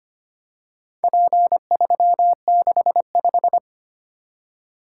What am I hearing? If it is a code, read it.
Morse code, heard as a keyed tone, about 25 wpm, 700 Hz. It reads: P365
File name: practice / 25wpm / Four Letter-Number Combinations - 25wpm